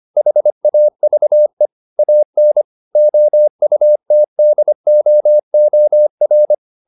Morse code, heard as a keyed tone, about 25 wpm, 600 Hz. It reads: HAVE AN OUTDOOR